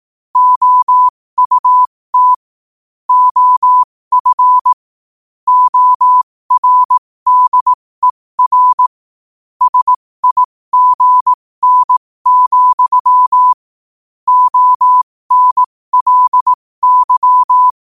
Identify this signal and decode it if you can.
Morse code, heard as a keyed tone, about 18 wpm, spaced as for 15 wpm, 1000 Hz. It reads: OUT OF ORDER SIGN, ONLY
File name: qotdfiles / 15wpm